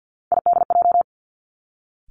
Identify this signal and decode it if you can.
Morse code, heard as a keyed tone, about 50 wpm, 700 Hz. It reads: IDJ
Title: Three Challenging Letter-Number Combinations 50wpm